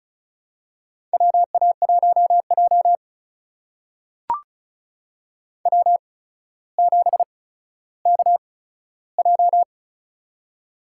Morse code, heard as a keyed tone, about 34 wpm, spaced as for 15 wpm, 700 Hz. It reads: WA1J W7KJ
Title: USA Call Signs - 4 Characters - 8x Character Spacing - 35wpm